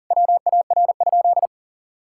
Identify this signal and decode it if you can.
Morse code, heard as a keyed tone, about 40 wpm, 700 Hz. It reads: WAR?
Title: Sentences from Top 1000 Words - Review Part 5 - 40wpm